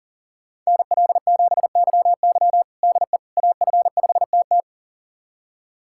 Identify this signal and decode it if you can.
Morse code, heard as a keyed tone, about 40 wpm, 700 Hz. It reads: NL7YY DE AF5TT
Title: USA Callsign DE Callsign - 5 Characters - 40wpm